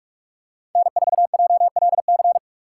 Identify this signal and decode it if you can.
Morse code, heard as a keyed tone, about 44 wpm, 690 Hz. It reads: N4JLC
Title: USA Call Signs - 5 Characters - 6x Character Spacing - 45wpm